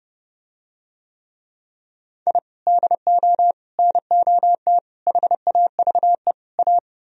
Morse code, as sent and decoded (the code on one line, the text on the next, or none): .. / -.. --- / -. --- - / .... .- ...- . / .-
I DO NOT HAVE A